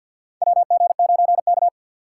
Morse code, heard as a keyed tone, about 53 wpm, 700 Hz. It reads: WG9K
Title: USA Call Signs - 6x Character Spacing - 50wpm